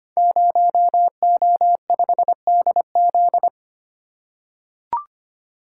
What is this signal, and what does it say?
Morse code, heard as a keyed tone, about 25 wpm, 700 Hz. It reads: 0O5DZ
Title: Code Groups - 5 Letter:Numbers - 25wpm